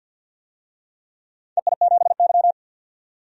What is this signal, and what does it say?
Morse code, heard as a keyed tone, about 47 wpm, 700 Hz. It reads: EI7Y
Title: DXCC-100 Call Signs - 4 Characters - 50wpm